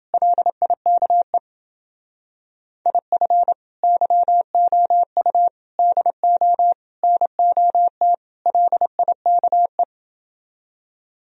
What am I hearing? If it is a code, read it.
Morse code, heard as a keyed tone, about 29 wpm, 700 Hz. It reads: LIKE IF YOU DO NOT LIKE